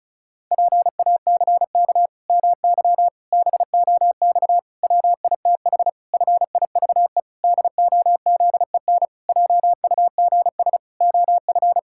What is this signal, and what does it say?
Morse code, heard as a keyed tone, about 36 wpm, 700 Hz. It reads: PACK MY BOX WITH FIVE DOZEN JUGS OF
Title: Pangram Challenge v2 - 35wpm